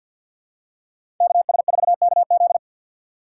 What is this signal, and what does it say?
Morse code, heard as a keyed tone, about 53 wpm, 700 Hz. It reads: KS4KZ
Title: USA Call Signs - 5 Characters - 4x Character Spacing - 50wpm